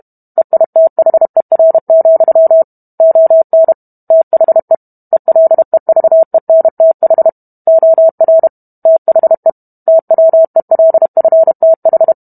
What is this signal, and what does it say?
Morse code, heard as a keyed tone, about 32 wpm, 650 Hz. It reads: EITHER, ON THE ELEVENTH OR THE TWELFTH